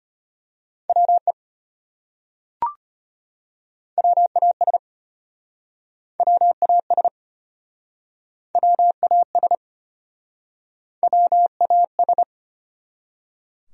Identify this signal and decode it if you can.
Morse code, heard as a keyed tone, about 33 wpm, 700 Hz. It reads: WE WAS WAS WAS WAS